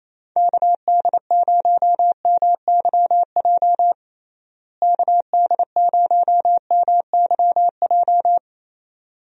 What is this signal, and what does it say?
Morse code, heard as a keyed tone, about 28 wpm, 700 Hz. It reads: KD0MYJ KD0MYJ
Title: USA Call Signs - 6 Characters - Repeated 2x - 28wpm